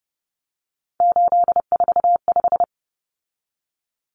Morse code, heard as a keyed tone, about 30 wpm, 700 Hz. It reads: 845
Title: Common Cut Numbers - 3-Digit 30wpm